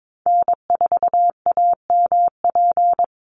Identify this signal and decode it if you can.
Morse code, heard as a keyed tone, about 22 wpm, 700 Hz. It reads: N4AMP
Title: USA Call Signs 22wpm